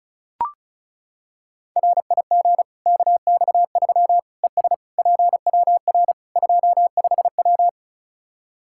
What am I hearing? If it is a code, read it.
Morse code, heard as a keyed tone, about 34 wpm, 700 Hz. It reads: RIG KX3 ES PWR 25W
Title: K1USN SST QSO Practice 35wpm